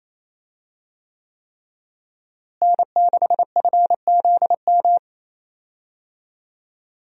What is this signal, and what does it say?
Morse code, heard as a keyed tone, about 28 wpm, 700 Hz. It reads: N6FZM
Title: USA Call Signs - 5 Characters - Repeated 3x - 28wpm